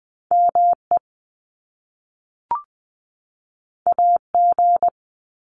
Morse code, heard as a keyed tone, about 20 wpm, 700 Hz. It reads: M E A G
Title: Top Two Letter Combinations in Words 20wpm